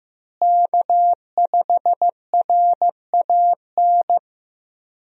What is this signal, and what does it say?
Morse code, heard as a keyed tone, about 15 wpm, 700 Hz. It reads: K5RAN
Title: USA Call Signs - 5 Characters - Repeated 3x - 15wpm